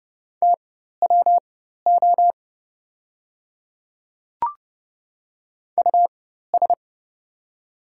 Morse code, heard as a keyed tone, about 31 wpm, spaced as for 14 wpm, 700 Hz. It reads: TWO US
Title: 4x Character Spacing - Top 200 Words - 30wpm